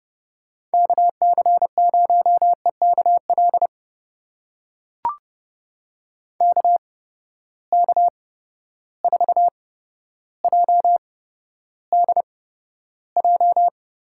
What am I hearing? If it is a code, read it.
Morse code, heard as a keyed tone, about 31 wpm, spaced as for 13 wpm, 700 Hz. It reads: KC0EKL KK4JDJ
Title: USA Call Signs - 6 Characters - 8x Character Spacing - 30wpm